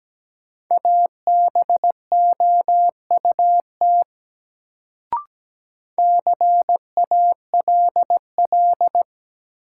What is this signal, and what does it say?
Morse code, heard as a keyed tone, about 17 wpm, 700 Hz. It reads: ABOUT CALL
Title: Sentences from Top 100 Words - New Word Drill - 17wpm